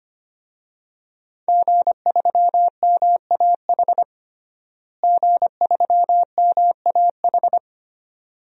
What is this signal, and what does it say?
Morse code, heard as a keyed tone, about 25 wpm, 700 Hz. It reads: G3MAH G3MAH
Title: DXCC-100 Call Signs - 5 Characters - Repeated 2x - 25wpm